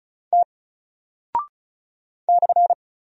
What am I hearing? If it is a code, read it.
Morse code, heard as a keyed tone, about 36 wpm, 700 Hz. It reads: T /
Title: Single Character (Letter:Number:Punctuation:Prosign) Alt 35wpm